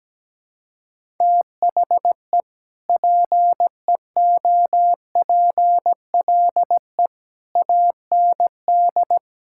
Morse code, heard as a keyed tone, about 17 wpm, 700 Hz. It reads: THE PEOPLE AND
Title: Sentences from Top 1600 Words 17wpm